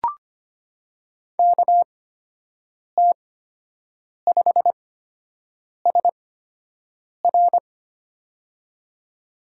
Morse code, heard as a keyed tone, about 25 wpm, spaced as for 7 wpm, 700 Hz. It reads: KT5SR
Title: USA Call Signs - 5 Characters - 8x Character Spacing - 25wpm